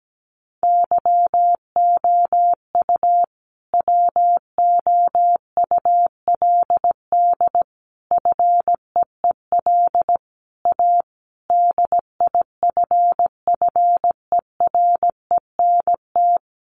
Morse code, heard as a keyed tone, about 17 wpm, 700 Hz. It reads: YOU WOULD FEEL A DIFFERENT
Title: Sentences from Top 600 Words 17wpm